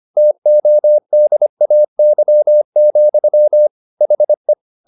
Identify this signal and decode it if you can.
Morse code, heard as a keyed tone, about 25 wpm, 600 Hz. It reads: TODAY, HE
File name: the old man and the sea part 1 25wpm